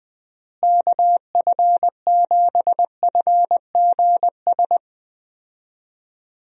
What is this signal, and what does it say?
Morse code, heard as a keyed tone, about 20 wpm, 700 Hz. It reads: KF7FGS
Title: USA Call Signs - 6 Characters - Repeated 3x - 20wpm